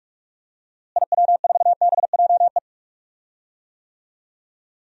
Morse code, heard as a keyed tone, about 46 wpm, 700 Hz. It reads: IW4BJE
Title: DXCC-100 Call Signs - 6 Characters - Repeated 2x - 45wpm